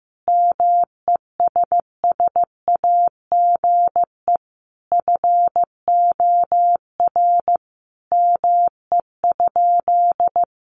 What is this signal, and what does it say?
Morse code, heard as a keyed tone, about 15 wpm, 700 Hz. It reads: MESSAGE FOR ME?
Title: Sentences from Top 1100 Words 15wpm